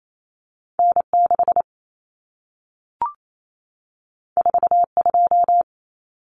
Morse code, heard as a keyed tone, about 28 wpm, 700 Hz. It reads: N6 42
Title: Common Cut Numbers - 2-Digit 28wpm